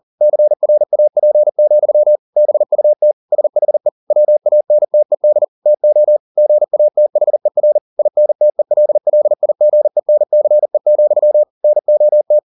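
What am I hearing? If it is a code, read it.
Morse code, heard as a keyed tone, about 40 wpm, 600 Hz. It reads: CRAP, BUT SHE WANTED TO GATHER INTELLIGENCE, NOT
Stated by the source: Morse code at 40 wpm, practice text, CW med Gnister.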